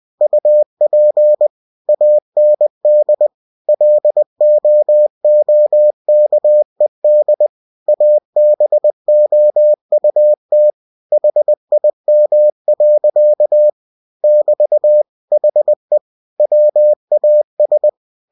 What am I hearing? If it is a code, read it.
Morse code, heard as a keyed tone, about 20 wpm, 600 Hz. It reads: UP AND LOOKED ABOUT HIM. <BT> HE WAS
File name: cw-books / winnie-the-pooh / winnie-20wpm-page-0053